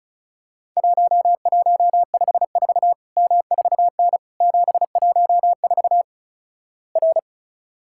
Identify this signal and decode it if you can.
Morse code, heard as a keyed tone, about 34 wpm, 700 Hz. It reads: 1154 M4N 714 R
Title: Worked All Europe DX Contest - QTC Exchange - 35wpm